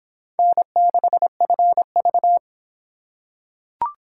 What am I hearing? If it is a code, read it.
Morse code, heard as a keyed tone, about 24 wpm, 700 Hz. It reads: N6FV